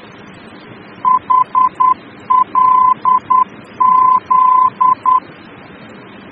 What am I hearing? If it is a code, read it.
Morse code, heard as a keyed tone, about 10 wpm, 1020 Hz. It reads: HLZ
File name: germany / mx / HLZ-D-VOR